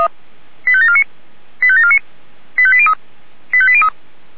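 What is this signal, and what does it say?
Five-tone selective call (ZVEI), heard as quick runs of five tones, tones of about 70 ms. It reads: 85749 85749 85803 85803
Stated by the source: Received on SDR; ZVEI five-tone call 85827, channel 464.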